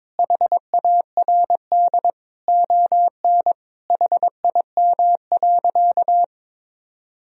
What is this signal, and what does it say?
Morse code, heard as a keyed tone, about 22 wpm, 700 Hz. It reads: HARD ON HIM.
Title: Sentences from Top 1000 Words - Review Part 1 - 22wpm